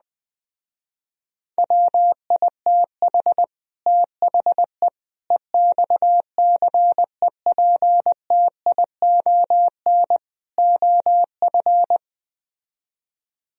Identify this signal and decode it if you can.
Morse code, heard as a keyed tone, about 20 wpm, 700 Hz. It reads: WITH THE EXCEPTION OF